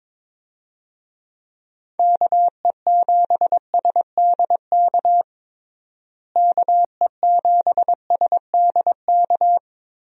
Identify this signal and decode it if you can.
Morse code, heard as a keyed tone, about 22 wpm, 700 Hz. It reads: KE7SDK KE7SDK
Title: USA Call Signs - 6 Characters - Repeated 2x - 22wpm